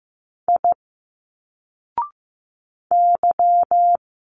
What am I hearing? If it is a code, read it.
Morse code, heard as a keyed tone, about 15 wpm, 700 Hz. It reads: I Y
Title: Single Letters 15wpm